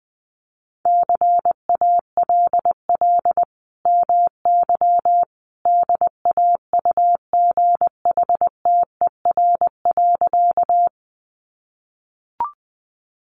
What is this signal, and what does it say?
Morse code, heard as a keyed tone, about 20 wpm, 700 Hz. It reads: CALL MY DAUGHTER.
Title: Sentences from Top 1500 Words 20wpm